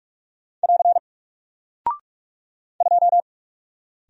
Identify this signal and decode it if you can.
Morse code, heard as a keyed tone, about 44 wpm, 700 Hz. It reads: <AR> 2
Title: Single Character (Letter:Number:Punctuation:Prosign) 45wpm